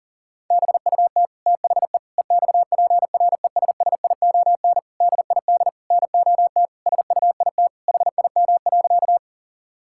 Morse code, heard as a keyed tone, about 37 wpm, 700 Hz. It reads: BUT THE EXPRESSION DID NOT SUIT HIM.
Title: Sentences from Top 1600 Words 40wpm